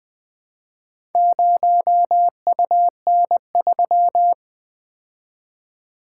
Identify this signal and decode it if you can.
Morse code, heard as a keyed tone, about 20 wpm, 700 Hz. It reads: 0UN3
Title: Four Letter-Number Combinations - 20wpm